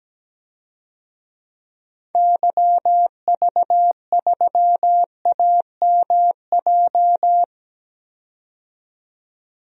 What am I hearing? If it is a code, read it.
Morse code, heard as a keyed tone, about 17 wpm, 700 Hz. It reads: YV3AMJ
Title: DXCC-100 Call Signs - 6 Characters - 17wpm